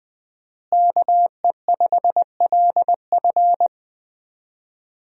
Morse code, heard as a keyed tone, about 20 wpm, 700 Hz. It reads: KE5LF
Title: USA Call Signs - 5 Characters - Repeated 2x - 20wpm